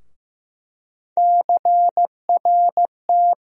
Morse code, heard as a keyed tone, about 15 wpm, 700 Hz. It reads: CRT
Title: USA Call Sign Postfixes - 3 Letters 15wpm